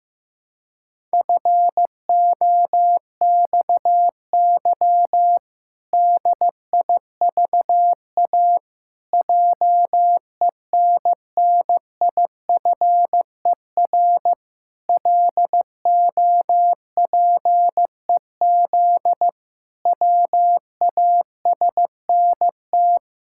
Morse code, heard as a keyed tone, about 15 wpm, 700 Hz. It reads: FOXY DIVA JENNIFER LOPEZ WASNT
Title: Pangram Challenge v2 - 15wpm